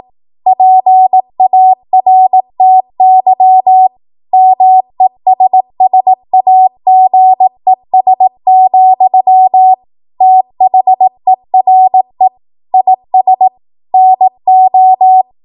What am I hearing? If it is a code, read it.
Morse code, heard as a keyed tone, about 18 wpm, 750 Hz. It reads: PARTY MESSAGES, THERE IS NO
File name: Morse / 18WPM